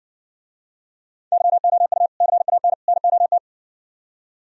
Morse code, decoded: YOU CAN NOT

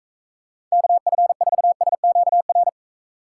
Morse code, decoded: KF4SQR

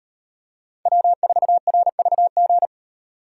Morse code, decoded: W4RVG

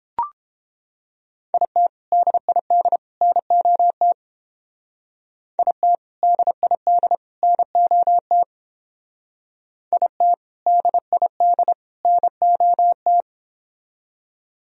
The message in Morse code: IT DID NOT IT DID NOT IT DID NOT

